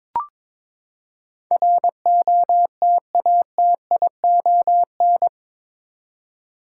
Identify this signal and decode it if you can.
Morse code, heard as a keyed tone, about 21 wpm, 700 Hz. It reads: ROTATION
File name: practice / 22wpm / Eight Letter Words - 22wpm